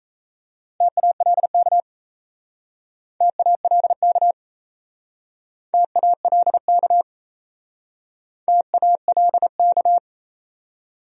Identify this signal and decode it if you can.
Morse code, heard as a keyed tone, about 38 wpm, 700 Hz. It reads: TALK TALK TALK TALK